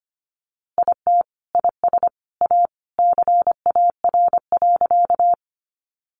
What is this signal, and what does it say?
Morse code, heard as a keyed tone, about 25 wpm, 700 Hz. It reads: IT IS A CAR.
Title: Easy Phrases 25wpm